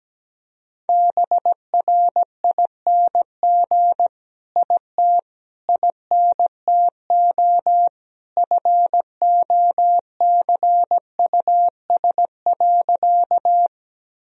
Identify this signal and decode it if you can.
Morse code, heard as a keyed tone, about 17 wpm, 700 Hz. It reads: BRING IT INTO FOCUS.
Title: Sentences from Top 1000 Words - Review Part 4 - 17wpm